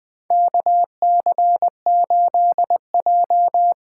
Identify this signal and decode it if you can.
Morse code, heard as a keyed tone, about 20 wpm, 700 Hz. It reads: KC8J
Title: USA Call Signs - 6 Characters - 8x Character Spacing - 20wpm